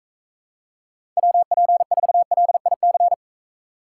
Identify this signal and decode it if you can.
Morse code, heard as a keyed tone, about 44 wpm, 700 Hz. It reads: WP4LIC